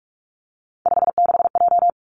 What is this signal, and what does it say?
Morse code, heard as a keyed tone, about 46 wpm, 700 Hz. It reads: 56J
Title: Three Challenging Letter-Number Combinations 45wpm